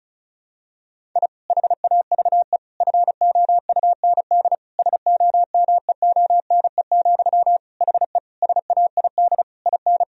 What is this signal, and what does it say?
Morse code, heard as a keyed tone, about 36 wpm, 700 Hz. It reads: I HAVE FOUND SOMEONE, HE SAID IN
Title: Sentences from Top 1000 Words - Review Part 1 - 35wpm